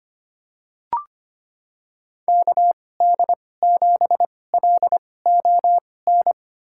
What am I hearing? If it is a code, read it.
Morse code, heard as a keyed tone, about 25 wpm, spaced as for 18 wpm, 700 Hz. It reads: KD7LON
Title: USA Call Signs - 6 Characters - 2x Character Spacing - 25wpm